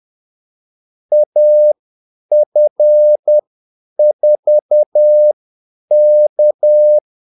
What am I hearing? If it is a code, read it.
Morse code, heard as a keyed tone, about 10 wpm, spaced as for 8 wpm, 600 Hz. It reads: AF4K
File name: Callsigns-10-8wpm